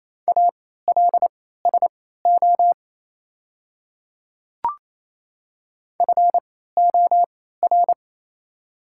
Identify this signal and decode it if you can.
Morse code, heard as a keyed tone, about 27 wpm, spaced as for 16 wpm, 690 Hz. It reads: ALSO FOR
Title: 3x Character Spacing - Top 300 Words - 28wpm